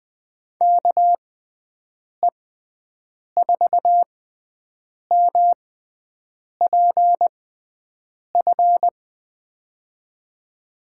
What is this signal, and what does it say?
Morse code, heard as a keyed tone, about 21 wpm, spaced as for 7 wpm, 700 Hz. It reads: KE4MPF
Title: USA Call Signs - 6x Character Spacing - 20wpm